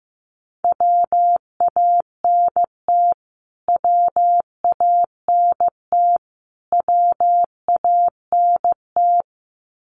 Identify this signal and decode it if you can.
Morse code, heard as a keyed tone, about 15 wpm, 700 Hz. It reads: WANT WANT WANT
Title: Top 200 Words Repeated 15wpm